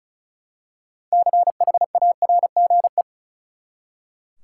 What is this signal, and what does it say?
Morse code, heard as a keyed tone, about 34 wpm, 700 Hz. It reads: CHARGE
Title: Six Letter Words - 35wpm